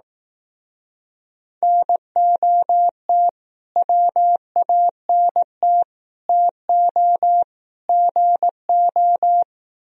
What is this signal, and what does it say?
Morse code, heard as a keyed tone, about 18 wpm, 700 Hz. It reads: NOT WANT TO GO
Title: Speed-Racing - Sets of 4 Words 15wpm